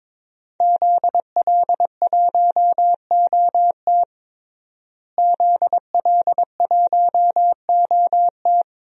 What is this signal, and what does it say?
Morse code, heard as a keyed tone, about 22 wpm, 700 Hz. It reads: ZL1OT ZL1OT
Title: DXCC-100 Call Signs - 5 Characters - Repeated 2x - 22wpm